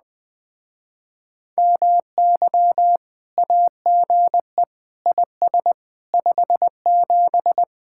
Morse code, heard as a keyed tone, about 20 wpm, 700 Hz. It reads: MY AGE IS 57
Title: Speed-Racing Common QSO Phrases - 17wpm